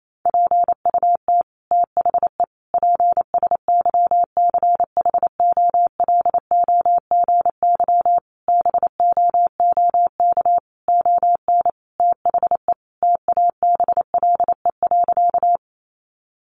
Morse code, heard as a keyed tone, about 28 wpm, 700 Hz. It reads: PUT THE PSYCHOLOGY BOOK ON THE TABLE.
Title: Sentences from Top 1400 Words 28wpm